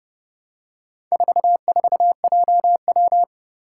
Morse code, heard as a keyed tone, about 29 wpm, 690 Hz. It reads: 44JW
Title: Four Challenging Letter-Number Combinations 30wpm